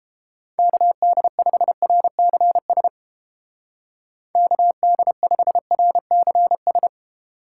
Morse code, heard as a keyed tone, about 31 wpm, 700 Hz. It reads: KD5RCS KD5RCS